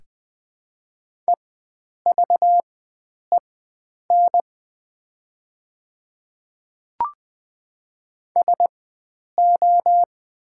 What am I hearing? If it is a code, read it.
Morse code, heard as a keyed tone, about 20 wpm, spaced as for 9 wpm, 700 Hz. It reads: EVEN SO
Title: 4x Character Spacing - Top 300 Words - 20wpm